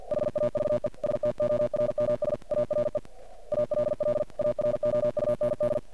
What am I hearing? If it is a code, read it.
Morse code, heard as a keyed tone, about 50 wpm, 600 Hz, with noise. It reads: HAVE STORMS ARE ALL AROUND
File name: archived / CRSnet